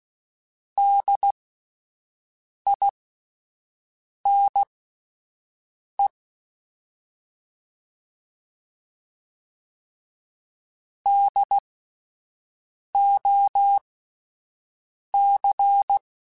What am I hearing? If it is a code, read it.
Morse code, heard as a keyed tone, about 16 wpm, spaced as for 5 wpm, 800 Hz. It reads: DINE DOC
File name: Spaced Alphas / Spaced Alphas mp3s / Ds sp @10